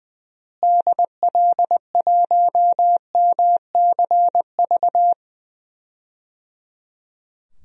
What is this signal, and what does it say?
Morse code, heard as a keyed tone, about 20 wpm, 700 Hz. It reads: DL1MCV
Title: DXCC-100 Call Signs - 6 Characters - Repeated 3x - 20wpm